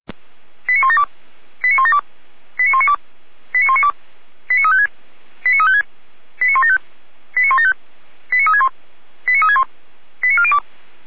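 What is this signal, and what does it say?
Five-tone selective call (ZVEI), heard as quick runs of five tones, tones of about 70 ms. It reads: 89172 89172 89192 89192 89356 89356 89175 89175 89361 89361 89402
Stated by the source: ZVEI five-tone selective call 89127 received by SDR, archived by Feuerwehr Orlamünde.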